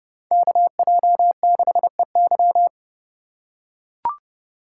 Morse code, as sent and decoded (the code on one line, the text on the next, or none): -.- .--- -.... -.--
KJ6EY